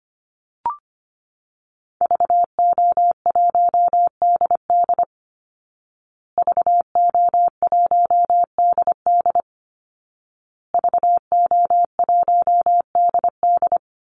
Morse code, heard as a keyed tone, about 25 wpm, 700 Hz. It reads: VO1DD VO1DD VO1DD
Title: Call Signs Repeated 2x - 25wpm